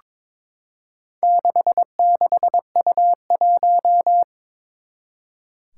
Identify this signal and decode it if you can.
Morse code, heard as a keyed tone, about 21 wpm, 700 Hz. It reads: 66U1
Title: Four Challenging Letter-Number Combinations 22wpm